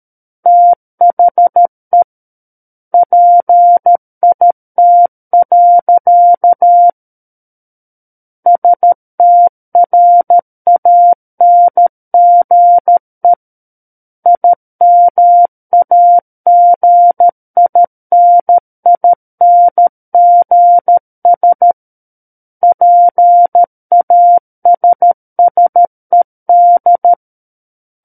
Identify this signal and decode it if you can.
Morse code, heard as a keyed tone, about 13 wpm, 700 Hz. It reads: THE PIT. STRANGE IMAGININGS PASSED
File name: Book One / 04-Chapter 4 - 13 WPM